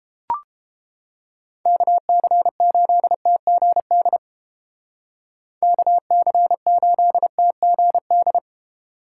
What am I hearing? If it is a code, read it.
Morse code, heard as a keyed tone, about 31 wpm, 700 Hz. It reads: KC8TGD KC8TGD